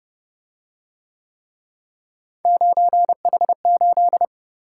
Morse code, heard as a keyed tone, about 29 wpm, 700 Hz. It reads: 9H8